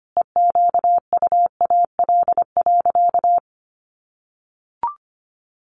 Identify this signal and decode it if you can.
Morse code, heard as a keyed tone, about 25 wpm, 700 Hz. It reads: EQUAL.
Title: Sentences from Top 1400 Words 25wpm